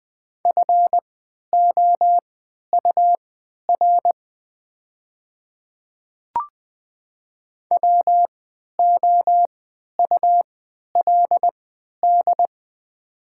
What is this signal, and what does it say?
Morse code, heard as a keyed tone, about 20 wpm, spaced as for 11 wpm, 700 Hz. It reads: FOUR WOULD
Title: 3x Character Spacing - Top 300 Words - 20wpm